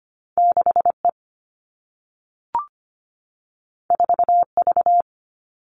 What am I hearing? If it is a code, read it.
Morse code, heard as a keyed tone, about 25 wpm, 700 Hz. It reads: 6E 4V
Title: All Cut Numbers - 2-Digit 25wpm